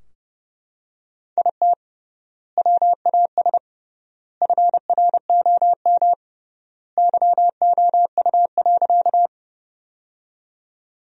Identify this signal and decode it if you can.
Morse code, heard as a keyed tone, about 30 wpm, 700 Hz. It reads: IT WAS FROM YOU.
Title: 3x Word-Spacing - Sentences from Top 100 Words - 30wpm